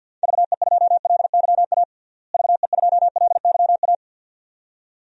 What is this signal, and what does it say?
Morse code, heard as a keyed tone, about 50 wpm, 700 Hz. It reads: VE2LYA VE2LYA
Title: DXCC-100 Call Signs - 6 Characters - Repeated 2x - 50wpm